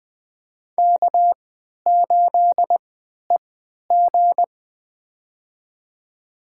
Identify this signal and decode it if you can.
Morse code, heard as a keyed tone, about 20 wpm, spaced as for 11 wpm, 700 Hz. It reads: K8EG
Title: USA Call Signs - 4 Characters - 3x Character Spacing - 20wpm